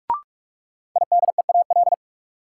Morse code, heard as a keyed tone, about 44 wpm, 700 Hz. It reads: IDEAL
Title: Sentences from Top 1400 Words - New Word Drill - 45wpm